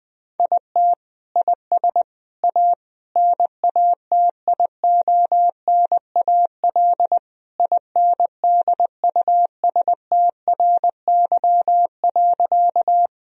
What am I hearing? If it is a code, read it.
Morse code, heard as a keyed tone, about 20 wpm, 700 Hz. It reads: IT IS A NATIONAL INDUSTRY.
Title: Sentences from Top 1000 Words - Review Part 4 - 20wpm